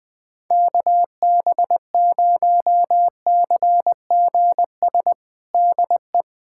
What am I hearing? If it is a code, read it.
Morse code, heard as a keyed tone, about 21 wpm, 700 Hz. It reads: KB0CGS DE
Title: USA Callsign DE Callsign - 6 Characters - 20wpm